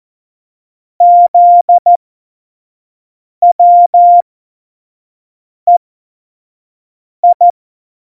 Morse code, heard as a keyed tone, about 14 wpm, spaced as for 5 wpm, 700 Hz. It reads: ZWEI
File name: Morsekurs / Witze / Witz 08. Dezember2020, 25 F70